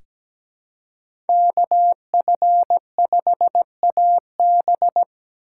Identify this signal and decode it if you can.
Morse code, heard as a keyed tone, about 17 wpm, 700 Hz. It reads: KF5AB
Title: USA Call Signs - 5 Characters - 6x Character Spacing - 17wpm